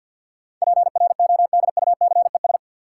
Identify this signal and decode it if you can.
Morse code, heard as a keyed tone, about 47 wpm, 700 Hz. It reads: PRODUCES
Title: Sentences from Top 1000 Words - Review Part 6 - New Word Drill - 50wpm